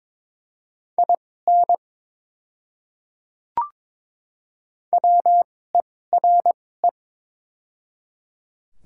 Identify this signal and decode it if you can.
Morse code, heard as a keyed tone, about 22 wpm, spaced as for 16 wpm, 700 Hz. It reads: IN WERE
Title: 2x Character Spacing - Top 200 Words - 22wpm